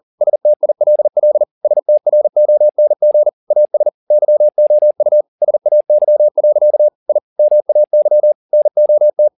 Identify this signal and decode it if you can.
Morse code, heard as a keyed tone, about 40 wpm, 600 Hz. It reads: STILL STRONG AS YOU SAY. I MAY NOT